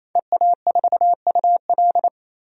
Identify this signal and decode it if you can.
Morse code, heard as a keyed tone, about 28 wpm, 700 Hz. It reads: EA4UL
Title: DXCC-100 Call Signs - 5 Characters - 28wpm